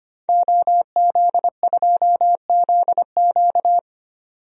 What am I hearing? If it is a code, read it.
Morse code, heard as a keyed tone, about 25 wpm, 700 Hz. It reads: OZ2ZQ
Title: DXCC-100 Call Signs - 5 Characters - 25wpm